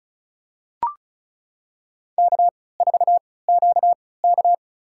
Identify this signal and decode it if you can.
Morse code, heard as a keyed tone, about 36 wpm, spaced as for 20 wpm, 700 Hz. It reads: K4QK
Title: USA Call Signs - 4 Characters - 3x Character Spacing - 35wpm